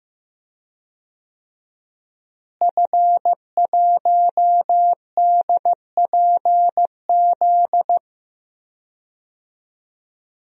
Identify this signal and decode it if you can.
Morse code, heard as a keyed tone, about 15 wpm, 700 Hz. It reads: F1DPZ
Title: DXCC-100 Call Signs - 5 Characters - Repeated 3x - 15wpm